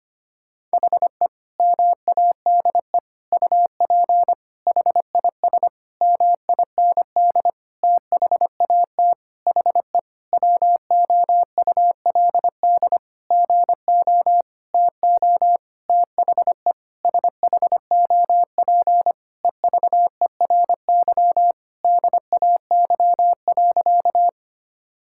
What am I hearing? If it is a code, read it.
Morse code, heard as a keyed tone, about 25 wpm, 700 Hz. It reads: HE MADE UP HIS MIND THAT HE WOULD GO TO THE SHOP EVERY DAY.
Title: Sentences from Top 1600 Words 25wpm